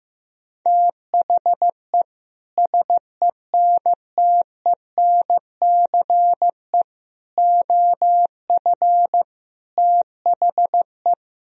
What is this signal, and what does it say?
Morse code, heard as a keyed tone, about 15 wpm, 700 Hz. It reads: THE SENTENCE OF THE
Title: Sentences from Top 1000 Words - Review Part 1 - 15wpm